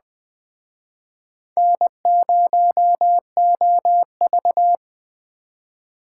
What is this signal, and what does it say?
Morse code, heard as a keyed tone, about 20 wpm, 700 Hz. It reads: N0OV
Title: Speed Racing - USA 1x2 Call Signs 17wpm